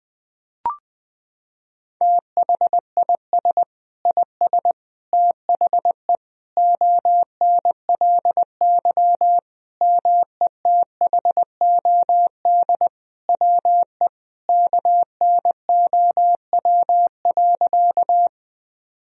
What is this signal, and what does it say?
Morse code, heard as a keyed tone, about 20 wpm, 700 Hz. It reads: THIS IS THE ONLY METHOD WE KNOW.
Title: Sentences from Top 1000 Words - Review Part 3 - 20wpm